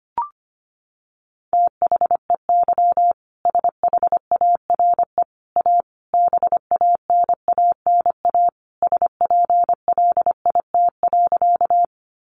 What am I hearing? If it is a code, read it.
Morse code, heard as a keyed tone, about 25 wpm, 700 Hz. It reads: THEY SHARE A BANANA SPLIT.
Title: Additional Easy Phrases 25wpm